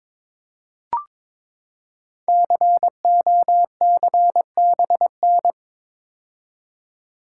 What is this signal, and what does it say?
Morse code, heard as a keyed tone, about 22 wpm, 700 Hz. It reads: COCBN
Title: Code Groups - 5 Letters - 22wpm